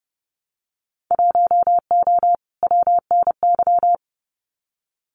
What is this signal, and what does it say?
Morse code, heard as a keyed tone, about 30 wpm, 700 Hz. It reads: 1O WNY
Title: Winter Field Day Exchanges Encore 30wpm